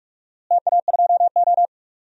Morse code, heard as a keyed tone, about 42 wpm, 700 Hz. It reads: TA2O
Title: DXCC-100 Call Signs - 4 Characters - Repeated 3x - 45wpm